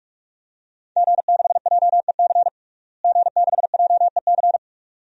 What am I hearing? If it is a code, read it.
Morse code, heard as a keyed tone, about 46 wpm, 700 Hz. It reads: G6JEC G6JEC
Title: DXCC-100 Call Signs - 5 Characters - Repeated 2x - 45wpm